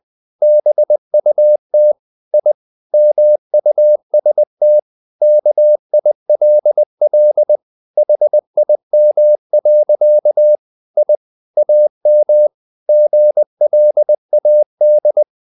Morse code, heard as a keyed tone, about 20 wpm, 600 Hz. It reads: BUT I MUST KILL HIM. I AM GLAD
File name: the old man and the sea part 2 20wpm